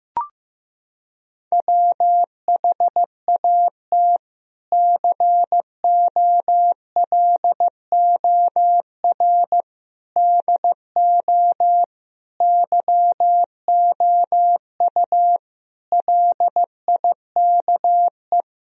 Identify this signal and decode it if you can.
Morse code, heard as a keyed tone, about 15 wpm, 700 Hz. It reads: WHAT COLOR DO YOU LIKE
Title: Sentences from Top 1000 Words - Review Part 2 - 15wpm